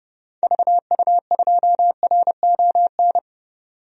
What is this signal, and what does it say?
Morse code, heard as a keyed tone, about 30 wpm, 700 Hz. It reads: VU2RON
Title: DXCC-100 Call Signs - 6 Characters - 30wpm